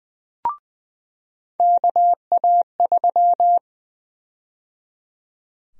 Morse code, heard as a keyed tone, about 20 wpm, 700 Hz. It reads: KA3
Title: USA Call Sign Prefixes - 20wpm